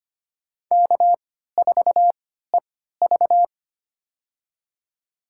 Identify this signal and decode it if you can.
Morse code, heard as a keyed tone, about 25 wpm, spaced as for 14 wpm, 700 Hz. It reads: K4EV
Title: USA Call Signs - 3x Character Spacing - 25wpm